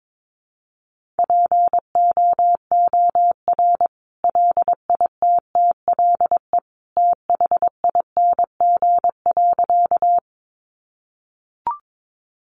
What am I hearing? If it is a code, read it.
Morse code, heard as a keyed tone, about 22 wpm, 700 Hz. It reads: POOR LITTLE THING.
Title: Sentences from Top 1000 Words 22wpm